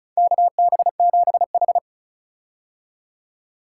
Morse code, heard as a keyed tone, about 34 wpm, 700 Hz. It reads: KB7H
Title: K1USN SST QSO Practice - 35wpm